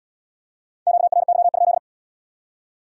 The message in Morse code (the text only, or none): QRY?